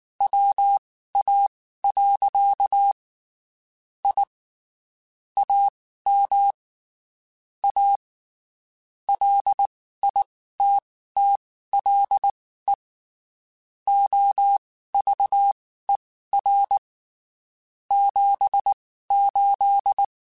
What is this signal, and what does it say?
Morse code, heard as a keyed tone, about 19 wpm, spaced as for 14 wpm, 800 Hz. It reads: WA. I AM A LITTLE OVER 78
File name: (spaced) / QSOs Spaced-mp3s / kc7sac @18 sp